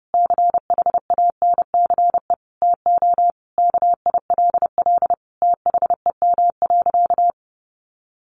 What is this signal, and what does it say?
Morse code, heard as a keyed tone, about 30 wpm, 700 Hz. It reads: CHANCE TO KILL THEM.
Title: Sentences from Top 1400 Words 30wpm